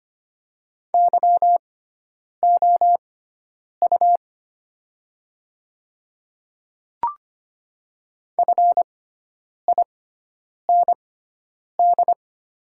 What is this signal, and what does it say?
Morse code, heard as a keyed tone, about 25 wpm, spaced as for 9 wpm, 690 Hz. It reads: YOU FIND
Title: 6x Character Spacing - Top 200 Words - 25wpm